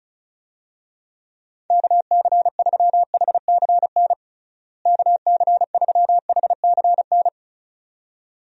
Morse code, heard as a keyed tone, about 34 wpm, 700 Hz. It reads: KC3HCN KC3HCN
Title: USA Call Signs - 6 Characters - Repeated 2x - 35wpm